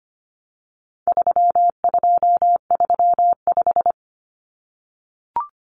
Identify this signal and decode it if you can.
Morse code, heard as a keyed tone, about 25 wpm, 700 Hz. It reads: 3235
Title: Numbers - 4-Digit 25wpm